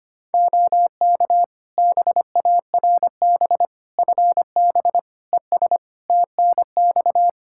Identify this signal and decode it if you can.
Morse code, heard as a keyed tone, about 25 wpm, 700 Hz. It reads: OK BARB FB ES TNX
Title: Practice QSO - Exchange 2 - Info - 25wpm